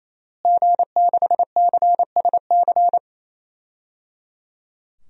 Morse code, heard as a keyed tone, about 28 wpm, 700 Hz. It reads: G6CSC
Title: DXCC-100 Call Signs - 5 Characters - Repeated 3x - 28wpm